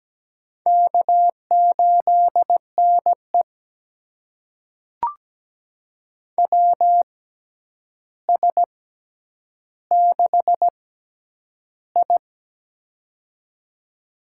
Morse code, heard as a keyed tone, about 17 wpm, spaced as for 9 wpm, 700 Hz. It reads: K8NE WS6I
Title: USA Call Signs - 4 Characters - 6x Character Spacing - 17wpm